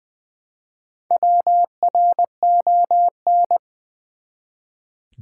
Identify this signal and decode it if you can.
Morse code, heard as a keyed tone, about 20 wpm, 700 Hz. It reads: WRON
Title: Build A Word - 1 Letter At A Time - 3 to 5-Letter Words - 20wpm